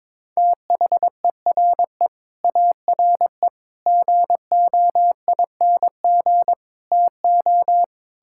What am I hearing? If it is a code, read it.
Morse code, heard as a keyed tone, about 21 wpm, 700 Hz. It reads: THERE ARE GOING TO
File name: practice / 22wpm / Sets of 8 Words - 22wpm